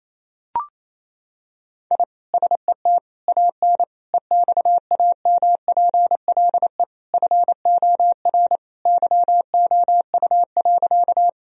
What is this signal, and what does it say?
Morse code, heard as a keyed tone, about 28 wpm, 700 Hz. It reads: I SET AN EXAMPLE FOR YOU.
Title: Sentences from Top 1000 Words - Review Part 1 - 28wpm